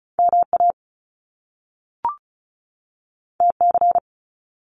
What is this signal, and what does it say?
Morse code, heard as a keyed tone, about 36 wpm, 700 Hz. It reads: MA TC
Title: Top Two Letter Combinations in Words 35wpm